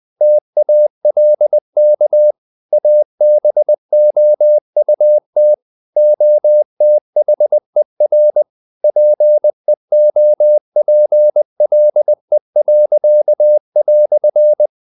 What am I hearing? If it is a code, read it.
Morse code, heard as a keyed tone, about 20 wpm, 600 Hz. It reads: TALK ABOUT OTHER PEOPLE."
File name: CW / Practice / EnglishQuotations0320